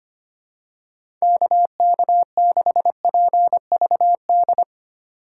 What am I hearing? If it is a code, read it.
Morse code, heard as a keyed tone, about 25 wpm, 700 Hz. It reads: KK6PVD
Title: USA Call Signs - 6 Characters - 3x Character Spacing - 25wpm